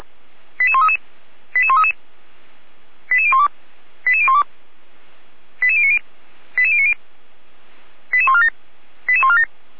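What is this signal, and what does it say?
Five-tone selective call (ZVEI), heard as quick runs of five tones, tones of about 80 ms. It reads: 88133 88133 88012 88012 88090 88090 88147 88147
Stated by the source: ZVEI five-tone call received on SDR, fire brigade Orlamünde (feuerwehr-orlamuende).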